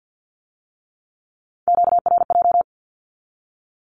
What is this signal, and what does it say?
Morse code, heard as a keyed tone, about 50 wpm, 700 Hz. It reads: QRJ
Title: Q-Codes with Spoken Meaning - 50wpm